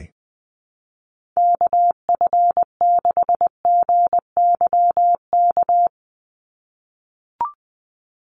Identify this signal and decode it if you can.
Morse code, heard as a keyed tone, about 20 wpm, 700 Hz. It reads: KF6GYK
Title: USA Call Signs 20wpm